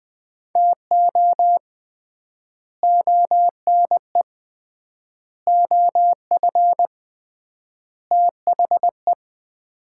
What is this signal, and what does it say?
Morse code, heard as a keyed tone, about 20 wpm, 700 Hz. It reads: TO ONE OF THE
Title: 3x Word-Spacing - Sets of 4 Words - 20wpm